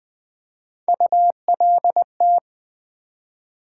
Morse code, heard as a keyed tone, about 20 wpm, 700 Hz. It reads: ULT
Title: Build A Word - 3 Letters At A Time - 8-Letter Words - 20wpm